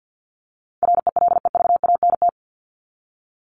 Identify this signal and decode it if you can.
Morse code, heard as a keyed tone, about 50 wpm, 700 Hz. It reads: RELEVANT
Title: Top 1001-1250 Words 50wpm